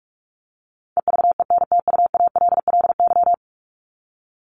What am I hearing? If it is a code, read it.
Morse code, heard as a keyed tone, about 44 wpm, 700 Hz. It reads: EVENTUALLY
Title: Top 751-1000 Words 45wpm